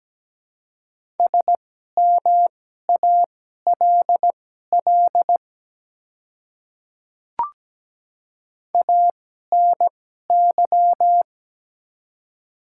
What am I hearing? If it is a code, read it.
Morse code, heard as a keyed tone, about 17 wpm, spaced as for 12 wpm, 700 Hz. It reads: SMALL ANY
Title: 2x Character Spacing - Top 200 Words - 17wpm